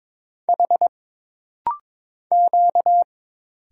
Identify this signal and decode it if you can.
Morse code, heard as a keyed tone, about 22 wpm, 700 Hz. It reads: H Q
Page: This rapid-fire timing is geared to push your ICR (Instant Character Recognition) skills.